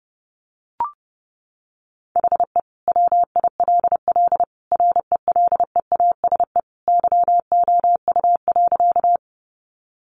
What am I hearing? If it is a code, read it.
Morse code, heard as a keyed tone, about 30 wpm, 700 Hz. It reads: HE WILL RELEASE YOU.
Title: Sentences from Top 1100 Words 30wpm